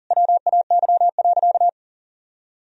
Morse code, heard as a keyed tone, about 40 wpm, 700 Hz. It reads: WAY.
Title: Sentences from Top 1000 Words - Review Part 1 - 40wpm